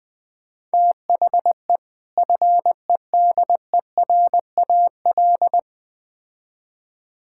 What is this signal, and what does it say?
Morse code, heard as a keyed tone, about 20 wpm, 700 Hz. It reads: THE FEDERAL
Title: Build A Phrase - 3-Word Phrases - 20wpm